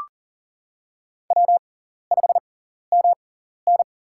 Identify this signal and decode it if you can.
Morse code, heard as a keyed tone, about 40 wpm, spaced as for 14 wpm, 700 Hz. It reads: W5MN
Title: USA Call Signs - 6x Character Spacing - 40wpm